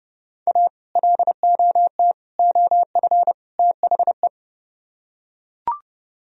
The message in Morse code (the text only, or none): A LOT OF THE